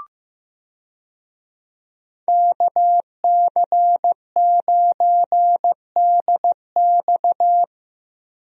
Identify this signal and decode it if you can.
Morse code, heard as a keyed tone, about 15 wpm, 700 Hz. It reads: KC9DX
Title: USA Call Signs - 5 Characters - Repeated 2x - 15wpm